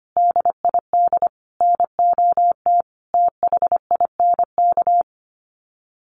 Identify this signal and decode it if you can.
Morse code, heard as a keyed tone, about 25 wpm, 700 Hz. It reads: DID NOT THINK
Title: Sets of 4 Words 25wpm